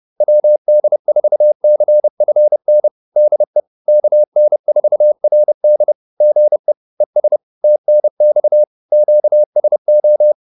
Morse code, heard as a keyed tone, about 30 wpm, 600 Hz. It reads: WD4CFN DE KN4RD GE ES TNX QSO
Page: advanced-practice-files